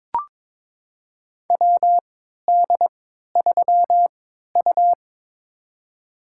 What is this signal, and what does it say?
Morse code, heard as a keyed tone, about 22 wpm, spaced as for 13 wpm, 700 Hz. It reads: WD3U
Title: USA Call Signs - 4 Characters - 3x Character Spacing - 22wpm